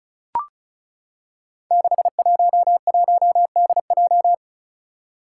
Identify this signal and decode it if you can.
Morse code, heard as a keyed tone, about 34 wpm, 700 Hz. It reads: 611DJ
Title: Five Challenging Letter-Number Combinations 35wpm